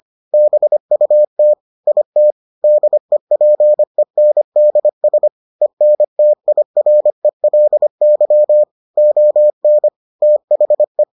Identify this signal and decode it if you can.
Morse code, heard as a keyed tone, about 25 wpm, 600 Hz. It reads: BUT IT DEPENDS ENTIRELY ON THE